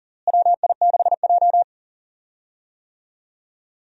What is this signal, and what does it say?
Morse code, heard as a keyed tone, about 40 wpm, 700 Hz. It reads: WI6J
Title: USA Call Signs - 4 Characters - Repeated 2x - 40wpm